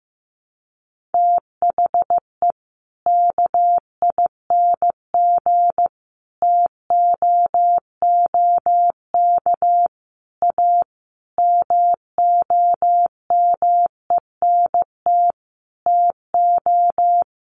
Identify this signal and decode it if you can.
Morse code, heard as a keyed tone, about 15 wpm, 700 Hz. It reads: THE KING TOOK A MOMENT TO
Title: Sentences from Top 1500 Words 15wpm